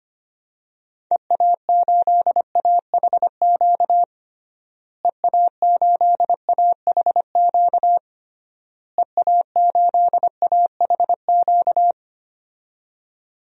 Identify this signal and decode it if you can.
Morse code, heard as a keyed tone, about 25 wpm, 700 Hz. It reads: EA8AHQ EA8AHQ EA8AHQ
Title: DXCC-100 Call Signs - 6 Characters - Repeated 3x - 25wpm